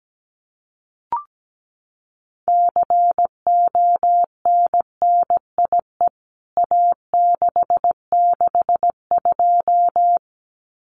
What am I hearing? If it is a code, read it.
Morse code, heard as a keyed tone, about 17 wpm, 700 Hz. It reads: CONNIE A662
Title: CWT Exchange 17wpm